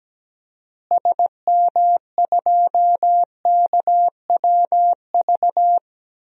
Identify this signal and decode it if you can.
Morse code, heard as a keyed tone, about 17 wpm, 700 Hz. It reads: SM2KWV
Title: DXCC-100 Call Signs - 6 Characters - Repeated 2x - 17wpm